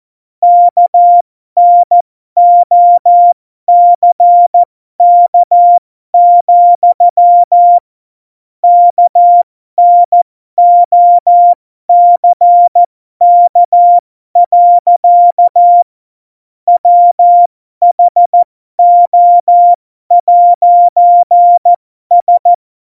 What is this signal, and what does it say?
Morse code, heard as a keyed tone, about 14 wpm, spaced as for 12 wpm, 700 Hz. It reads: KNOCK, KNOCK. WHO'S
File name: Morsekurs / Witze